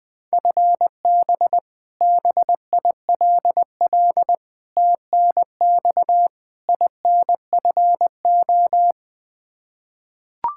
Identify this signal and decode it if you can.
Morse code, heard as a keyed tone, about 20 wpm, 700 Hz. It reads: FB BILL TNX INFO
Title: Practice QSO - Exchange 3 - More Info - 20wpm